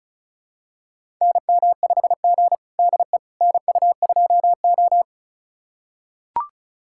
Morse code, heard as a keyed tone, about 36 wpm, 700 Hz. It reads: NM5G DE NU2O
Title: USA Callsign DE Callsign - 4 Characters - 35wpm